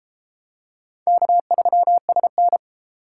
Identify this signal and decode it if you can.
Morse code, heard as a keyed tone, about 31 wpm, 700 Hz. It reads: K3SN